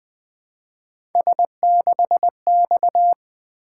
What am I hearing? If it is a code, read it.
Morse code, heard as a keyed tone, about 20 wpm, 700 Hz. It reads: S6X
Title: Three Letter-Number Combinations - 20wpm